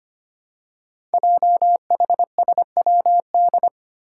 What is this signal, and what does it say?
Morse code, heard as a keyed tone, about 25 wpm, 700 Hz. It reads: JHSWD
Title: Code Groups - 5 Letters - 25wpm